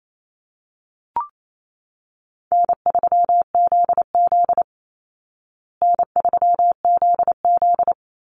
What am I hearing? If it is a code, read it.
Morse code, heard as a keyed tone, about 28 wpm, 700 Hz. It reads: N3ZZ N3ZZ
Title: Call Signs Repeated 3x - 28wpm